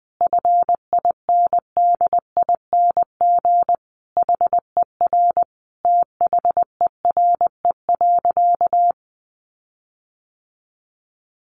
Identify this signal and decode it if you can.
Morse code, heard as a keyed tone, about 20 wpm, 700 Hz. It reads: FINDING HER THERE.
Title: Sentences from Top 1500 Words 20wpm